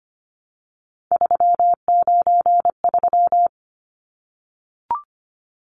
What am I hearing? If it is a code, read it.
Morse code, heard as a keyed tone, about 25 wpm, 700 Hz. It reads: 393
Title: Numbers - 3-Digit 25wpm